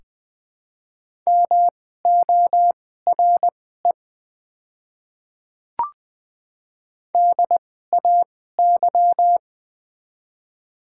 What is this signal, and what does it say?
Morse code, heard as a keyed tone, about 20 wpm, spaced as for 14 wpm, 690 Hz. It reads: MORE DAY
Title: 2x Character Spacing - Top 100 Words - 20wpm